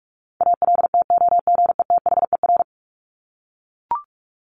Morse code, heard as a keyed tone, about 44 wpm, 700 Hz. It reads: ALTOGETHER
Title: Top 1501-1750 Words 45wpm